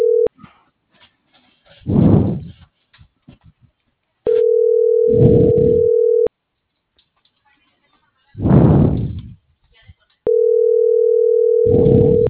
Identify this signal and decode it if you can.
Ringback tone (North America), 440 + 480 Hz.